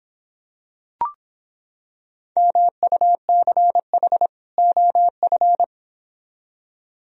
Morse code, heard as a keyed tone, about 26 wpm, 700 Hz. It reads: MUCH OF